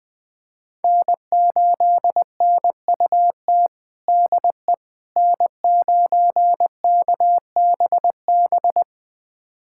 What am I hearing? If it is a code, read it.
Morse code, heard as a keyed tone, about 20 wpm, 700 Hz. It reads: N8NUT DE N9KBB
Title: USA Callsign DE Callsign - 5 Characters - 20wpm